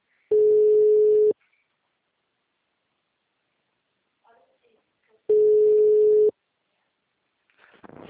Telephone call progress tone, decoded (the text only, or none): ringback tone